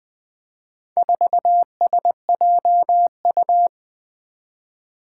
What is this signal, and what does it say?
Morse code, heard as a keyed tone, about 20 wpm, 700 Hz. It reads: 4SJU
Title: Four Challenging Letter-Number Combinations 20wpm